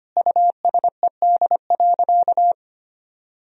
Morse code, heard as a keyed tone, about 25 wpm, 700 Hz. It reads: USED.
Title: Sentences from Top 1000 Words - Review Part 4 - 25wpm